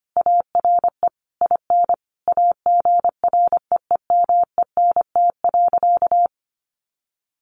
Morse code, heard as a keyed tone, about 25 wpm, 700 Hz. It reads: ARE IN AGREEMENT.
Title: Sentences from Top 1500 Words 25wpm